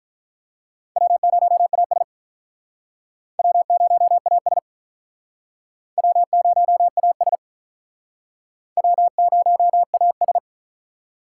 Morse code, heard as a keyed tone, about 42 wpm, 700 Hz. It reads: W0AS W0AS W0AS W0AS
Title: Speed Racing - USA 1x2 Call Signs 35wpm